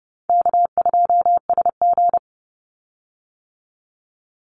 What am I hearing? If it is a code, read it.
Morse code, heard as a keyed tone, about 31 wpm, 700 Hz. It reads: K2SG
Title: Easy Call Signs - 30wpm